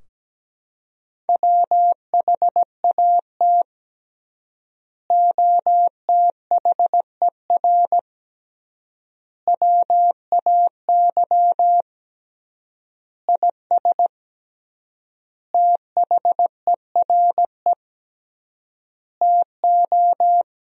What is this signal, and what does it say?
Morse code, heard as a keyed tone, about 17 wpm, 700 Hz. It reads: WHAT OTHER WAY IS THERE TO
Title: 3x Word-Spacing - Sentences from Top 100 Words - 17wpm